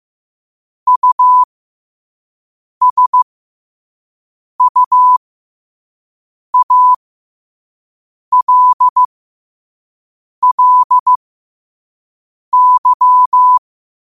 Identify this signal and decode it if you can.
Morse code, heard as a keyed tone, about 15 wpm, spaced as for 5 wpm, 1000 Hz. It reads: USUALLY
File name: qotdfiles / 5wpm